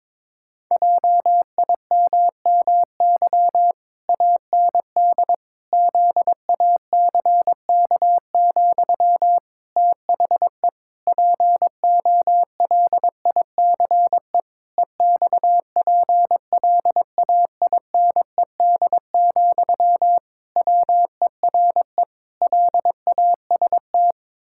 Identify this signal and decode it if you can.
Morse code, heard as a keyed tone, about 22 wpm, 700 Hz. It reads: JIMMY AND ZACK, THE POLICE EXPLAINED, WERE LAST
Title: Pangram Challenge v2 - 22wpm